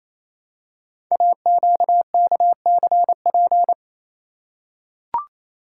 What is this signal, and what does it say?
Morse code, heard as a keyed tone, about 27 wpm, 690 Hz. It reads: AQKCP
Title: Code Groups - 5 Letters - 28wpm